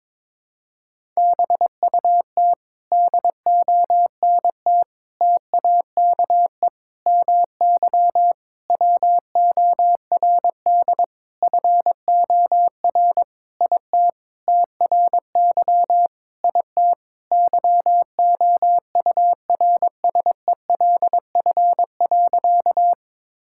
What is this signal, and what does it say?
Morse code, heard as a keyed tone, about 22 wpm, 700 Hz. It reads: BUT DONT TAKE MY WORD FOR IT TRY IT YOURSELF.
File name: Sentences with 7-Word Phrases - 22wpm